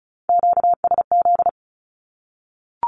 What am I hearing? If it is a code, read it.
Morse code, heard as a keyed tone, about 34 wpm, 700 Hz. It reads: QSZ
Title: Q-Codes Repeated 3x 35wpm